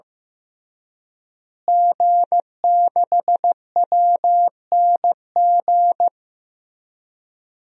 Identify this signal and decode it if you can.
Morse code, heard as a keyed tone, about 16 wpm, 700 Hz. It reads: G6WNG
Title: DXCC-100 Call Signs - 5 Characters - Repeated 3x - 15wpm